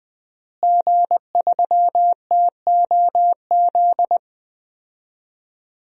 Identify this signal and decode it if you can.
Morse code, heard as a keyed tone, about 20 wpm, 700 Hz. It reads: G3TOZ
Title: DXCC-100 Call Signs - 5 Characters - Repeated 3x - 20wpm